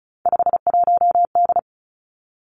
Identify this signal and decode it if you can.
Morse code, heard as a keyed tone, about 36 wpm, 700 Hz. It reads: 51D
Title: Three Challenging Letter-Number Combinations 35wpm